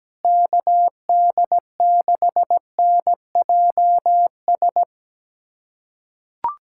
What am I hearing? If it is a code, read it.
Morse code, heard as a keyed tone, about 17 wpm, 700 Hz. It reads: KD6NJS
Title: USA Callsign DE Callsign 17wpm